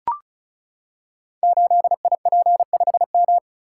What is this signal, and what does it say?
Morse code, heard as a keyed tone, about 36 wpm, 700 Hz. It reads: 8IP5M
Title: Code Groups - 5 Letter:Numbers - 35wpm